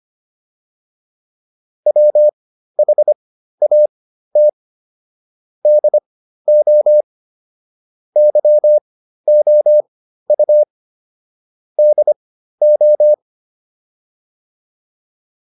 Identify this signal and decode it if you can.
Morse code, heard as a keyed tone, about 25 wpm, spaced as for 13 wpm, 600 Hz. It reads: WHAT DO YOU DO